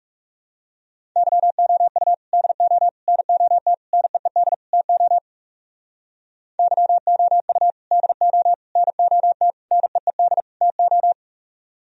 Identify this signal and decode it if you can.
Morse code, heard as a keyed tone, about 38 wpm, 700 Hz. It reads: YOU DO NOT NEED TO YOU DO NOT NEED TO